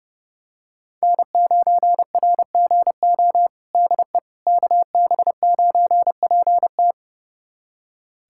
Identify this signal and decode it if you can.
Morse code, heard as a keyed tone, about 32 wpm, 700 Hz. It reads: N9RGO DE KB9PT
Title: USA Callsign DE Callsign - 5 Characters - 30wpm